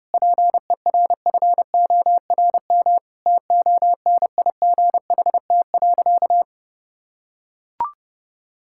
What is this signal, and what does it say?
Morse code, heard as a keyed tone, about 30 wpm, 700 Hz. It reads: PERFORM TONIGHT.
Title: Sentences from Top 1000 Words - Review Part 5 - 30wpm